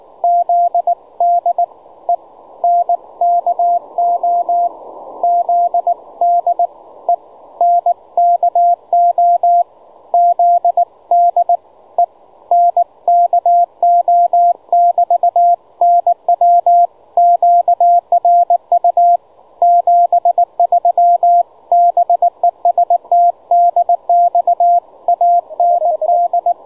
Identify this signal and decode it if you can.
Morse code, heard as a keyed tone, about 19 wpm, 700 Hz, with radio noise. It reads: ZDENKO ZDENKO ZDENKO<BT> NW QRU 73 BEST DX AND
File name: juma-trx2 / juma-trx2-30m-cw-sample